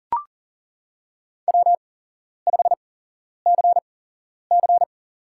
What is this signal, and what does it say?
Morse code, heard as a keyed tone, about 40 wpm, spaced as for 11 wpm, 700 Hz. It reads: W5CC
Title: USA Call Signs - 8x Character Spacing - 40wpm